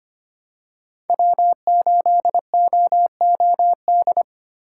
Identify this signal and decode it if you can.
Morse code, heard as a keyed tone, about 25 wpm, 700 Hz. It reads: W8OOD
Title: USA Call Signs - 5 Characters - 3x Character Spacing - 25wpm